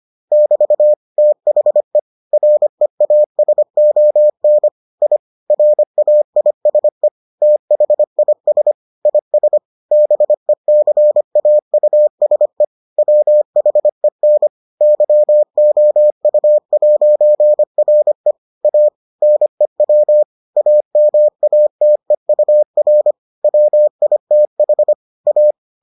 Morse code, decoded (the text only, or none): <BT> THE REASON I RAISE THIS IS BECAUSE WHEN YOU'RE A NEW AMATEUR WITH A